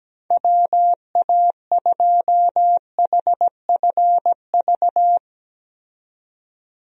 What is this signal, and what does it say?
Morse code, heard as a keyed tone, about 17 wpm, 700 Hz. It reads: WA2HFV
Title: USA Call Signs - 6 Characters - 17wpm